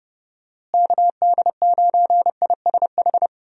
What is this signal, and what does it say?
Morse code, heard as a keyed tone, about 32 wpm, 700 Hz. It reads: KD9ISH
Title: USA Call Signs - 6 Characters - Repeated 3x - 30wpm